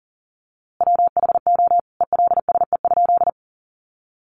Morse code, heard as a keyed tone, about 40 wpm, 700 Hz. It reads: WHO ELSE?
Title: A Sound Of Thunder 40wpm